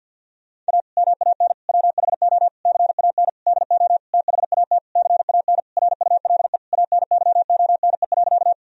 Morse code, heard as a keyed tone, about 50 wpm, 700 Hz. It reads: A MAN WHO CAN DO THAT CAN RULE ANYONE.
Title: Sentences from Top 1000 Words - Review Part 2 - 50wpm